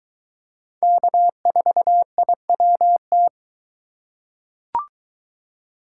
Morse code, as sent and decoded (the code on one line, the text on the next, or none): -.- ....- .. .-- -
K4IWT